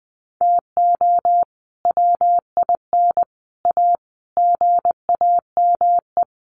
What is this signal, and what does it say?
Morse code, heard as a keyed tone, about 20 wpm, 700 Hz. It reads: TO WIN A GAME
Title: Sentences from Top 1500 Words 20wpm